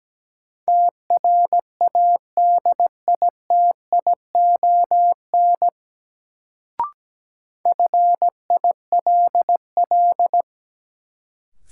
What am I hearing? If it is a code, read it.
Morse code, heard as a keyed tone, about 17 wpm, 700 Hz. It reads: TRADITION FILL
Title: Sentences from Top 1500 Words - New Word Drill Alt - 17wpm